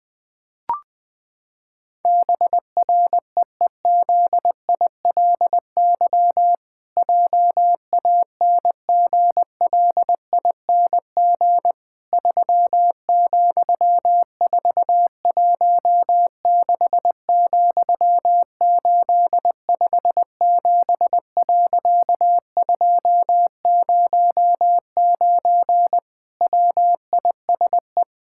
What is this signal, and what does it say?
Morse code, heard as a keyed tone, about 21 wpm, 700 Hz. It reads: BREEZILY JANGLING 3,416,857.209 WISE
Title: Pangram Challenge v2 - 20wpm